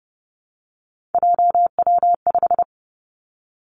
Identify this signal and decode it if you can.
Morse code, heard as a keyed tone, about 30 wpm, 700 Hz. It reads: JW5
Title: Three Challenging Letter-Number Combinations 30wpm